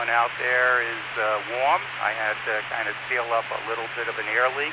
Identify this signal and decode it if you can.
Emergency Alert System (SAME) tones, heard as harsh buzzing bursts, with speech. It is heard from 0.5 to 4.3 s.